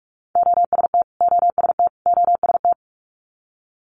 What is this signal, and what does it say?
Morse code, heard as a keyed tone, about 46 wpm, 700 Hz. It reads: OST OST OST
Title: Multi-letter Phonemes Repeated 3x 45wpm